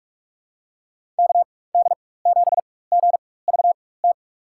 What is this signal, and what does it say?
Morse code, heard as a keyed tone, about 46 wpm, spaced as for 21 wpm, 700 Hz. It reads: KD7GVT
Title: USA Call Signs - 4x Character Spacing - 45wpm